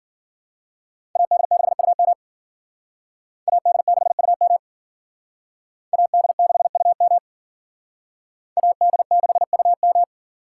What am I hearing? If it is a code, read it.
Morse code, heard as a keyed tone, about 52 wpm, 700 Hz. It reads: AD6UM AD6UM AD6UM AD6UM